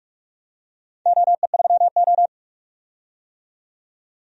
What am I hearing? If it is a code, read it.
Morse code, heard as a keyed tone, about 44 wpm, 700 Hz. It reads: OE3O
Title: DXCC-100 Call Signs - 4 Characters - 45wpm